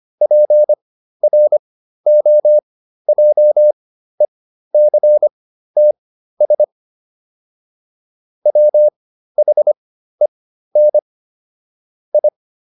Morse code, heard as a keyed tone, about 25 wpm, spaced as for 13 wpm, 600 Hz. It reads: PROJECTS WHEN I